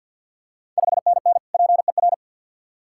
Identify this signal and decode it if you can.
Morse code, heard as a keyed tone, about 52 wpm, 700 Hz. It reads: 5NN PER
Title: Ontario QSO Party Exchange 50wpm